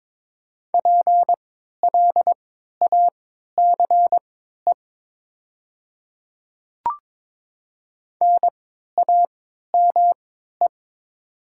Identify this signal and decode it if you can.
Morse code, heard as a keyed tone, about 22 wpm, spaced as for 12 wpm, 700 Hz. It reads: PLACE NAME
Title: 3x Character Spacing - Top 300 Words - 22wpm